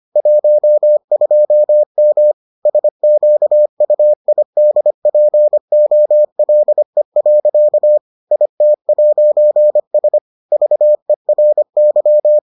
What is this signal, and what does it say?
Morse code, heard as a keyed tone, about 25 wpm, 600 Hz. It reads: 12M SQUIDPOLE. IT'S VERY